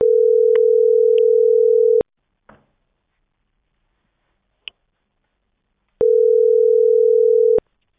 Ringback tone (North America), 440 + 480 Hz.
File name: outgoing recordings